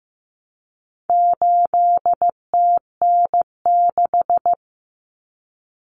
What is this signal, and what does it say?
Morse code, heard as a keyed tone, about 15 wpm, 700 Hz. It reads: 8TN6
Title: Common Cut Numbers - 4-Digit 15wpm